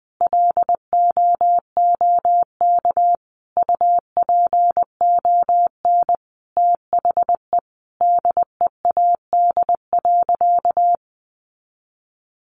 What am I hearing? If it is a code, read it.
Morse code, heard as a keyed tone, about 19 wpm, 700 Hz. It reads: LOOK UPON THE DEAD.
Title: Sentences from Top 1500 Words 20wpm